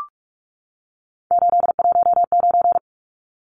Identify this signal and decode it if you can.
Morse code, heard as a keyed tone, about 44 wpm, 700 Hz. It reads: 819
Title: Numbers - 3-Digit 45wpm